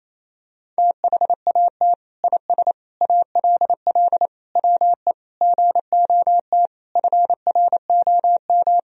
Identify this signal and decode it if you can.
Morse code, heard as a keyed tone, about 28 wpm, 700 Hz. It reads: THAT IS ALL WE GOT FROM
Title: Sentences from Top 1000 Words - Review Part 2 - 28wpm